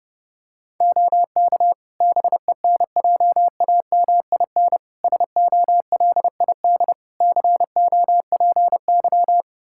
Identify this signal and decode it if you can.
Morse code, heard as a keyed tone, about 29 wpm, 700 Hz. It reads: OK BENJAMIN SOLID COPY
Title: Practice QSO - Exchange 3 - More Info - 30wpm